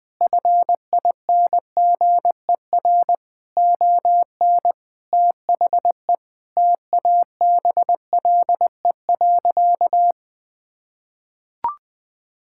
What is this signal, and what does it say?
Morse code, heard as a keyed tone, about 20 wpm, 700 Hz. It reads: FINGER ON THE TABLE.
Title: Sentences from Top 1600 Words 20wpm